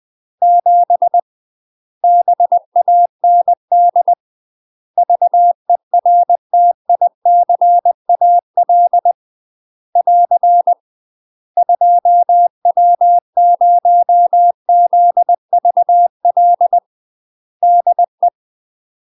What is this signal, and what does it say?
Morse code, heard as a keyed tone, about 20 wpm, 700 Hz. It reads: 7 BAND VERTICAL <AR> 2W0ZVL DE
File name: UK MASTER QSOs TRAINING / 20WPM / UK MASTER QSOs 0045